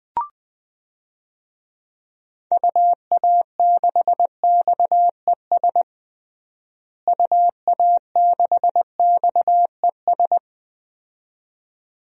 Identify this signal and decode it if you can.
Morse code, heard as a keyed tone, about 20 wpm, 700 Hz. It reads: UA6XES UA6XES
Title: DXCC-100 Call Signs - 6 Characters - Repeated 2x - 20wpm